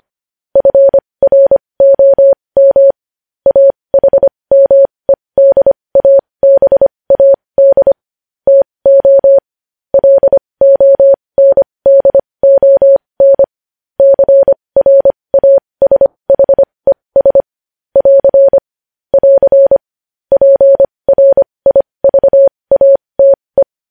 Morse code, heard as a keyed tone, about 25 wpm, spaced as for 20 wpm, 550 Hz. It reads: FROM AHMEDABAD TO LONDON CRASHES <AR> <AR> PRIVATE
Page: news-headlines-in-morse-code-at-20-wpm